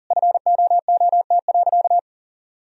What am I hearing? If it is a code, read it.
Morse code, heard as a keyed tone, about 40 wpm, 700 Hz. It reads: FOOT.
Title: Sentences from Top 1000 Words - Review Part 3 - 40wpm